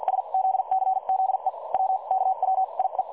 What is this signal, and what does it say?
Morse code, heard as a keyed tone, about 50 wpm, spaced as for 40 wpm, 760 Hz. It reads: SZCZE WKWIE
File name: dlazaawansowanych / nagraniazpasma / websdr recording 3565.0kHz